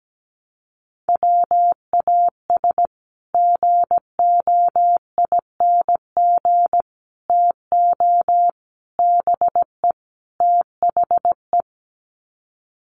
Morse code, heard as a keyed tone, about 17 wpm, 700 Hz. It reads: WAS GOING TO BE THE
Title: Sets of 5 Words Encore2 17wpm